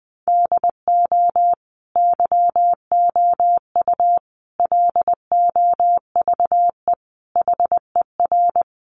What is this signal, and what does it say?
Morse code, heard as a keyed tone, about 20 wpm, 700 Hz. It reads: DO YOU LOVE HER
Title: Sentences from Top 400 Words 20wpm